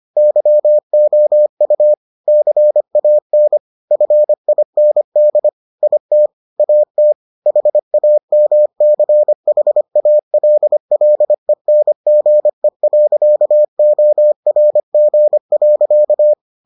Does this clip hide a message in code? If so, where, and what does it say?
Yes: Morse code from 0.2 to 16.3 s, heard as a keyed tone, about 25 wpm, 600 Hz. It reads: YOU CAN FIND IT AT HAMCHALLENGE.ORG.